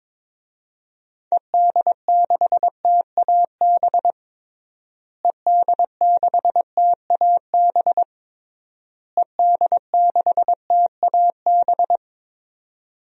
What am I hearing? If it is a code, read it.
Morse code, heard as a keyed tone, about 22 wpm, 700 Hz. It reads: ED6TAB ED6TAB ED6TAB
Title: DXCC-100 Call Signs - 6 Characters - Repeated 3x - 22wpm